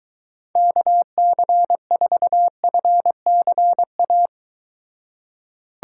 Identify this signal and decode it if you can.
Morse code, heard as a keyed tone, about 23 wpm, 700 Hz. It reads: KC4FCA